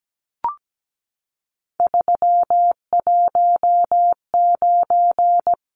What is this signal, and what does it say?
Morse code, heard as a keyed tone, about 17 wpm, 700 Hz. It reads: 319
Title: Numbers - 3-Digit 17wpm